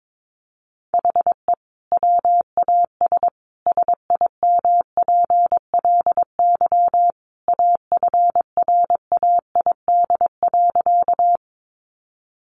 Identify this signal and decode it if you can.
Morse code, heard as a keyed tone, about 22 wpm, 700 Hz. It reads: HE WAS SIMPLY AFRAID.
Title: Sentences from Top 1500 Words 22wpm